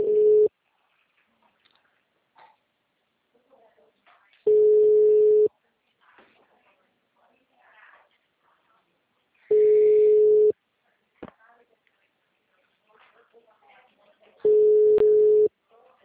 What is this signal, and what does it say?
Ringback tone (Europe, 425 Hz), with speech.